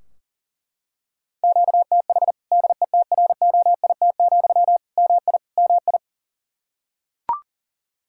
Morse code, heard as a keyed tone, about 40 wpm, 700 Hz. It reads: QTH DETROIT, MI MI
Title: Practice QSO - Exchange 1 - Beginning - 40wpm